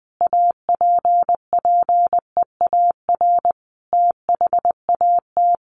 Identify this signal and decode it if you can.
Morse code, heard as a keyed tone, about 20 wpm, 700 Hz. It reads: APPEAR THAT
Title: Sentences from Top 1400 Words 20wpm